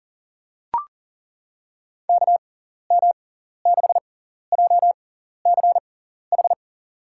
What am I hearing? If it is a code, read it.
Morse code, heard as a keyed tone, about 40 wpm, spaced as for 14 wpm, 700 Hz. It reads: KM6JCH
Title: USA Call Signs - 6 Characters - 6x Character Spacing - 40wpm